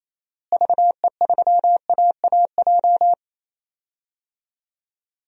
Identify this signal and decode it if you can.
Morse code, heard as a keyed tone, about 28 wpm, 700 Hz. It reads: VE3AAJ
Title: DXCC-100 Call Signs - 6 Characters - Repeated 3x - 28wpm